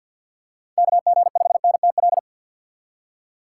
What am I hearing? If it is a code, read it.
Morse code, heard as a keyed tone, about 53 wpm, 690 Hz. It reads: KG5NTL
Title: USA Call Signs - 2x Character Spacing - 50wpm